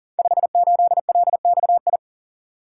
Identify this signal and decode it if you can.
Morse code, heard as a keyed tone, about 40 wpm, 700 Hz. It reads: 58LXI
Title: Code Groups - 5 Letter:Numbers - 40wpm